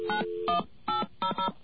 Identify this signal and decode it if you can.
DTMF, heard as telephone touch tones, tones of about 120 ms. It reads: 91800
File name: mp3files / off the hook